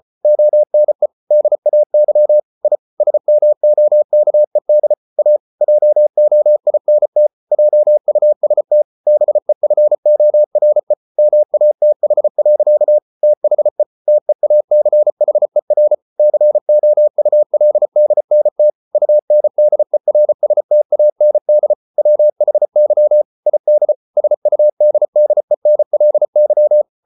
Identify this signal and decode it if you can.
Morse code, heard as a keyed tone, about 34 wpm, 600 Hz. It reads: ONE DAY I SMOKED A JOINT JUST BEFORE MATH. THE TEACHER COULDNT UNDERSTAND WHY ID SUDDENLY